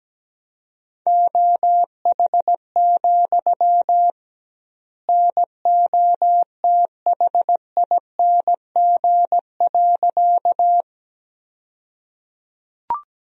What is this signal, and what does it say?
Morse code, heard as a keyed tone, about 17 wpm, 700 Hz. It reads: OH, NOTHING.
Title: 2x Word-Spacing - Sentences from Top 500 Words - 17wpm